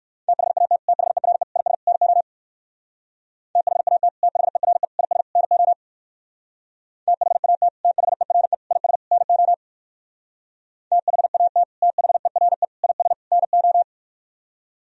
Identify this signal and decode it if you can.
Morse code, heard as a keyed tone, about 59 wpm, 700 Hz. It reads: THAT THERE IS NO THAT THERE IS NO THAT THERE IS NO THAT THERE IS NO